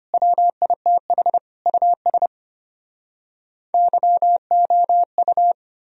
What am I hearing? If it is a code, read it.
Morse code, heard as a keyed tone, about 33 wpm, 700 Hz. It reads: WITH US YOU